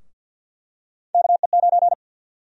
Morse code, heard as a keyed tone, about 50 wpm, 700 Hz. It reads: KE9
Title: USA Call Sign Prefixes - Repeated 2x - 50wpm